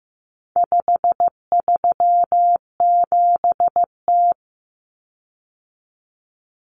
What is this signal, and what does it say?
Morse code, heard as a keyed tone, about 15 wpm, 700 Hz. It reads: 537T
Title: Common Cut Numbers - 4-Digit 15wpm